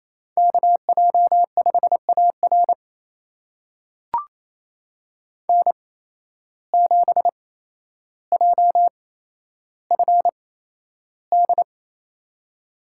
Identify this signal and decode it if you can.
Morse code, heard as a keyed tone, about 28 wpm, spaced as for 12 wpm, 700 Hz. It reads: KJ5AR N7JFD
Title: USA Call Signs - 5 Characters - 8x Character Spacing - 28wpm